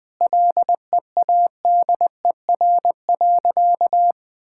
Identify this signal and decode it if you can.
Morse code, heard as a keyed tone, about 20 wpm, 700 Hz. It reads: LEADER.
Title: Sentences with 6-Word Phrases - 20wpm